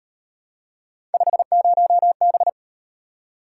Morse code, heard as a keyed tone, about 38 wpm, 700 Hz. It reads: 50B